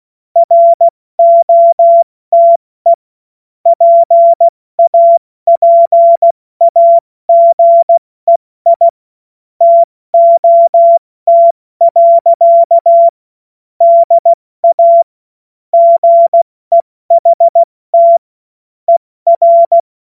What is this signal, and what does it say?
Morse code, heard as a keyed tone, about 16 wpm, 680 Hz. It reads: ROTE PAPAGEI TOT. DA GEHT ER
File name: Morsekurs / Witze / Witz 28. September 2021, 70 F80